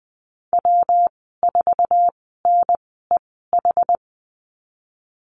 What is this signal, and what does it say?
Morse code, heard as a keyed tone, about 20 wpm, spaced as for 14 wpm, 700 Hz. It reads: W4NEH
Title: 154 Lesson 25g - Letter p - Call Signs 20wpm